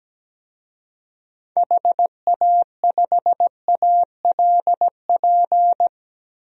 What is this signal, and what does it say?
Morse code, heard as a keyed tone, about 17 wpm, 700 Hz. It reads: HA5ALP
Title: DXCC-100 Call Signs - 6 Characters - 17wpm